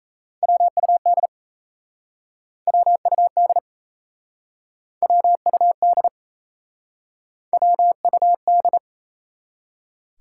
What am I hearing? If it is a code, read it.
Morse code, heard as a keyed tone, about 37 wpm, 700 Hz. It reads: WUD WUD WUD WUD